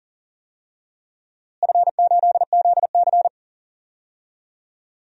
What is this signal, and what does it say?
Morse code, heard as a keyed tone, about 40 wpm, 700 Hz. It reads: F8ZC
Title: DXCC-100 Call Signs - 4 Characters - Repeated 3x - 40wpm